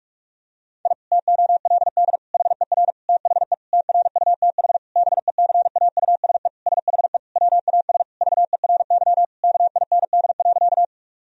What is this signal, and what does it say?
Morse code, heard as a keyed tone, about 46 wpm, 700 Hz. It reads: I TOLD HER THE TRUTH BECAUSE SHE WAS VERY KIND.
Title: Sentences from Top 1000 Words - Review Part 3 - 45wpm